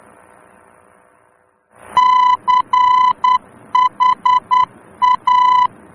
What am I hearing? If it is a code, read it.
Morse code, heard as a keyed tone, about 10 wpm, 1000 Hz. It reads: CHA